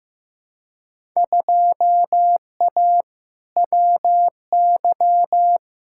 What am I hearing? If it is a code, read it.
Morse code, heard as a keyed tone, about 15 wpm, 700 Hz. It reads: 2A WY
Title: Field Day Exchanges - 15wpm